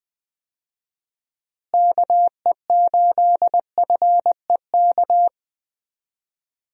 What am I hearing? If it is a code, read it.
Morse code, heard as a keyed tone, about 20 wpm, 700 Hz. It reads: KE8FEK
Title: USA Call Signs - 6 Characters - Repeated 3x - 20wpm